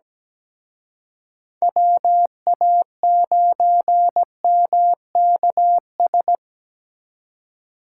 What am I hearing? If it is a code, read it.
Morse code, heard as a keyed tone, about 20 wpm, 700 Hz. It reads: WA9MKS
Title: Speed Racing - USA 2x3 Call Signs 17wpm